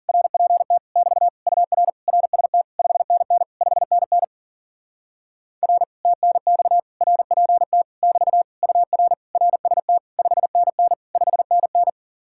Morse code, heard as a keyed tone, about 40 wpm, 700 Hz. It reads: RPT <BT> UR RST 5NN 5NN R TNX RPT <BT> UR RST 5NN 5NN
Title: Speed-Racing Common QSO Phrases Encore - 40wpm